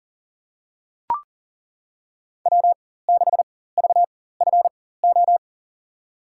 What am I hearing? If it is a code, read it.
Morse code, heard as a keyed tone, about 40 wpm, spaced as for 19 wpm, 700 Hz. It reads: W6VFO
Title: USA Call Signs - 5 Characters - 4x Character Spacing - 40wpm